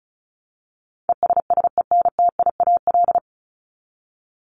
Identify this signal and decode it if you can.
Morse code, heard as a keyed tone, about 34 wpm, 700 Hz. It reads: ESSENTIAL
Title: LY Suffix - Top 100 Words with Base Words 35wpm